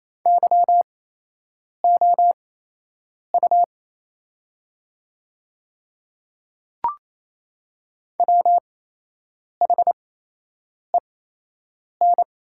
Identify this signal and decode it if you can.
Morse code, heard as a keyed tone, about 27 wpm, spaced as for 8 wpm, 700 Hz. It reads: YOU WHEN
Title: 8x Character Spacing - Top 200 Words - 28wpm